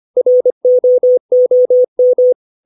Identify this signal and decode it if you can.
Morse code, heard as a keyed tone, about 25 wpm, 500 Hz. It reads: ROOM
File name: 632 words / 632 Words 25WPM